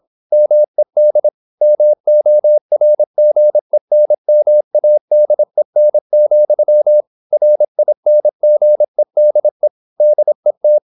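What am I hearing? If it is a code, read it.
Morse code, heard as a keyed tone, about 26 wpm, 600 Hz. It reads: MED MORGENMADEN, RINGEDE DET
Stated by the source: High-speed (QRQ) Morse at 26 wpm.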